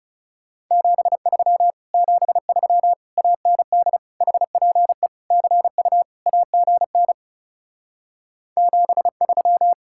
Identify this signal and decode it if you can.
Morse code, heard as a keyed tone, about 39 wpm, 700 Hz. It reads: 73 73 AND HPE CU AGN 73